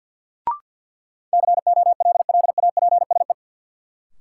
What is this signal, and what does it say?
Morse code, heard as a keyed tone, about 53 wpm, 700 Hz. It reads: COLLAPSE
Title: Sets of 5 Words Encore - New Word Drill - 50wpm